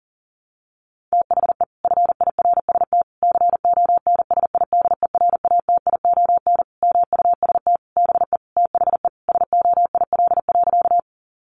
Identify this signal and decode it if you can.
Morse code, heard as a keyed tone, about 40 wpm, 700 Hz. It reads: THE FIRST CONSIDERATION MUST BE THE SOIL.
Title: Sentences from Top 1500 Words 40wpm